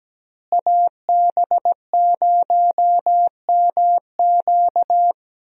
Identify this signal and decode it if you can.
Morse code, heard as a keyed tone, about 17 wpm, 700 Hz. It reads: AB0MQ
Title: USA Call Signs - 5 Characters - Repeated 2x - 17wpm